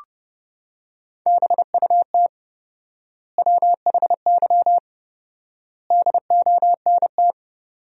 Morse code, heard as a keyed tone, about 30 wpm, 700 Hz. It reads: BUT WHY DONT
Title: 4x Word-Spacing - Sentences from Top 300 Words - 30wpm